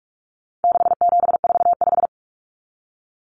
Morse code, heard as a keyed tone, about 44 wpm, 700 Hz. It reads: 6745
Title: Common Cut Numbers - 4-Digit 45wpm